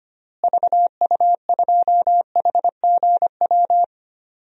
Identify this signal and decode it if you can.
Morse code, heard as a keyed tone, about 25 wpm, 700 Hz. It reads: VU2HGW
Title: DXCC-100 Call Signs - 6 Characters - Repeated 2x - 25wpm